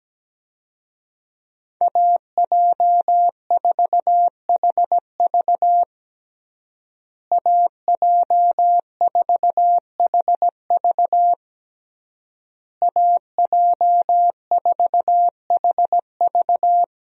Morse code, heard as a keyed tone, about 17 wpm, 700 Hz. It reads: AJ4HV AJ4HV AJ4HV
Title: USA Call Signs - 5 Characters - Repeated 3x - 17wpm